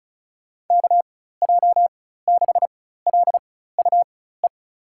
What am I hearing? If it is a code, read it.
Morse code, heard as a keyed tone, about 36 wpm, spaced as for 16 wpm, 700 Hz. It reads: KJ6LUE
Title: USA Call Signs - 6 Characters - 4x Character Spacing - 35wpm